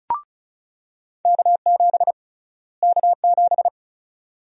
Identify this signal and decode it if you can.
Morse code, heard as a keyed tone, about 34 wpm, 700 Hz. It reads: K7 K7
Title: USA Call Sign Prefixes - Repeated 2x - 35wpm